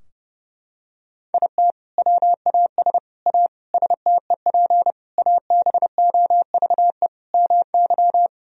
Morse code, heard as a keyed tone, about 30 wpm, 700 Hz. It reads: IT WAS A STEP ABOVE MY
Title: Sentences from Top 1000 Words - Review Part 3 - 30wpm